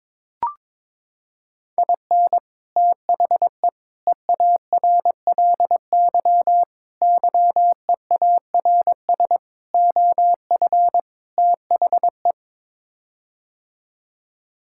Morse code, heard as a keyed tone, about 22 wpm, 700 Hz. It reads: IN THE EARLY YEARS OF THE
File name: practice / 22wpm / Sets of 6 Words - 22wpm